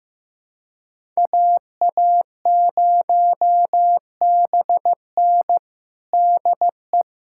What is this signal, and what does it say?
Morse code, heard as a keyed tone, about 15 wpm, 700 Hz. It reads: AA0BN DE
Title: USA Callsign DE Callsign - 5 Characters - 15wpm